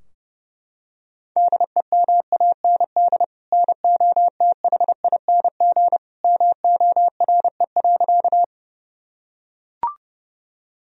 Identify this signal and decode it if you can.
Morse code, heard as a keyed tone, about 30 wpm, 700 Hz. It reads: DEMAND NOTHING MORE.
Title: Sentences from Top 1000 Words - Review Part 4 - 30wpm